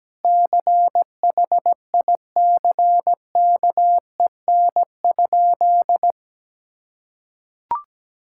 Morse code, heard as a keyed tone, about 17 wpm, 700 Hz. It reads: CHICKEN?
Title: Sentences from Top 1000 Words - Review Part 5 - 17wpm